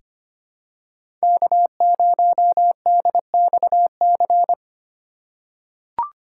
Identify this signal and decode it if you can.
Morse code, heard as a keyed tone, about 25 wpm, 700 Hz. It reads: K0DXC
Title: K1USN SST QSO Practice - 25wpm